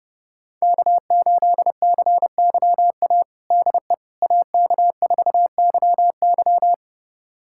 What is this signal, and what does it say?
Morse code, heard as a keyed tone, about 30 wpm, 700 Hz. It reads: K8CYA DE AK4YY
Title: USA Callsign DE Callsign - 5 Characters - 30wpm